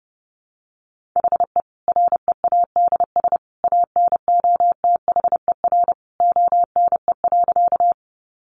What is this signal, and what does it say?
Morse code, heard as a keyed tone, about 30 wpm, 700 Hz. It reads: HE READS ANOTHER ONE.
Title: Additional Easy Phrases 30wpm